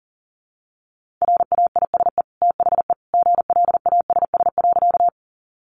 Morse code, heard as a keyed tone, about 40 wpm, 700 Hz. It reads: RAISE THE GLASS.
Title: Sentences from Top 1400 Words 40wpm